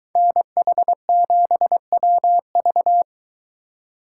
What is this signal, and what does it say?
Morse code, heard as a keyed tone, about 22 wpm, 700 Hz. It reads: NH7WV